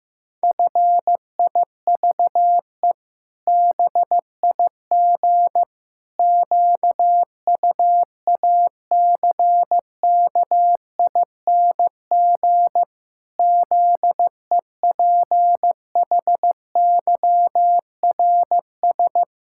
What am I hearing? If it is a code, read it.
Morse code, heard as a keyed tone, about 15 wpm, 700 Hz. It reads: FIVE BIG QUACKING ZEPHYRS
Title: Pangram Challenge v2 - 15wpm